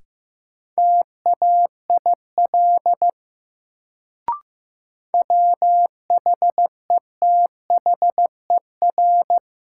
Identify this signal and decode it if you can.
Morse code, heard as a keyed tone, about 16 wpm, 700 Hz. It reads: TAIL WHETHER
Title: Sentences from Top 1000 Words - Review Part 3 - New Word Drill - 15wpm